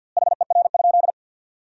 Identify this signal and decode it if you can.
Morse code, heard as a keyed tone, about 50 wpm, 700 Hz. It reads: HER?
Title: 2x Word-Spacing - Sentences from Top 100 Words - 50wpm